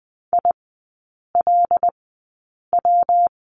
Morse code, heard as a keyed tone, about 20 wpm, spaced as for 8 wpm, 700 Hz. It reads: ILW
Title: 127 Lesson 21e - Number 6 - Three Characters 20wpm